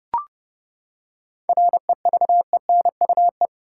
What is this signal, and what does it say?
Morse code, heard as a keyed tone, about 30 wpm, 700 Hz. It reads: REVENUE
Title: Seven Letter Words - 30wpm